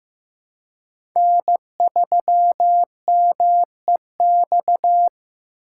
Morse code, heard as a keyed tone, about 15 wpm, 700 Hz. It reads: N3MEX
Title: USA Call Signs - 5 Characters - 3x Character Spacing - 15wpm